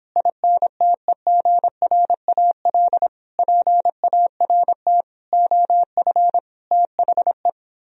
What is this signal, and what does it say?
Morse code, heard as a keyed tone, about 29 wpm, 700 Hz. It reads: INTEGRAL PART OF THE